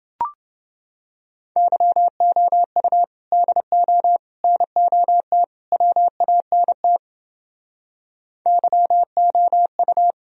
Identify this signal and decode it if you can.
Morse code, heard as a keyed tone, about 28 wpm, 700 Hz. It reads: YOU DO NOT WANT YOU